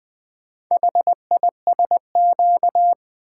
Morse code, heard as a keyed tone, about 25 wpm, 700 Hz. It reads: HISQ
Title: Speed-Racing - Four Letter-Number Combinations - 20wpm